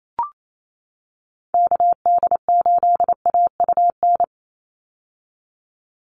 Morse code, heard as a keyed tone, about 28 wpm, 700 Hz. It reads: KD8AUN
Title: USA Call Signs 28wpm